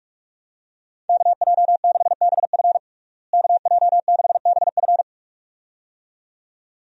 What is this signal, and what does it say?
Morse code, heard as a keyed tone, about 46 wpm, 700 Hz. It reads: KJ6BF KJ6BF
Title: USA Call Signs - 5 Characters - Repeated 2x - 45wpm